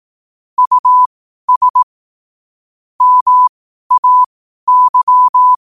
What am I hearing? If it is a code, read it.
Morse code, heard as a keyed tone, about 18 wpm, spaced as for 12 wpm, 1000 Hz. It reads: US MAY
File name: qotdfiles / 12wpm